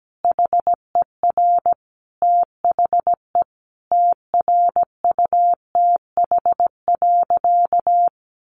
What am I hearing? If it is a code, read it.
Morse code, heard as a keyed tone, about 17 wpm, 700 Hz. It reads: HER THE TRUTH.
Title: Sentences from Top 800 Words 17wpm